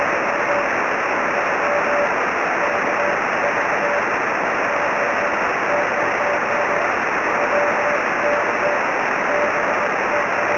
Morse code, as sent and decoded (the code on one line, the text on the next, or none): none